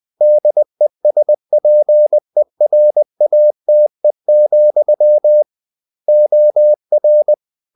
Morse code, heard as a keyed tone, about 20 wpm, 600 Hz. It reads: DESPERATE, OR
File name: the old man and the sea part 2 20wpm